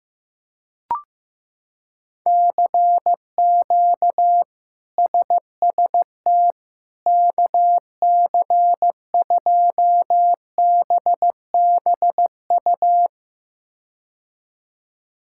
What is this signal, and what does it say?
Morse code, heard as a keyed tone, about 16 wpm, 700 Hz. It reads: CQ SST KC2BBU
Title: K1USN SST QSO Practice - 15wpm